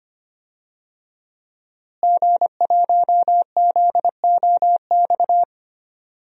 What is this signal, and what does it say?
Morse code, heard as a keyed tone, about 25 wpm, 700 Hz. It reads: G1ZOX
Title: DXCC-100 Call Signs - 5 Characters - Repeated 3x - 25wpm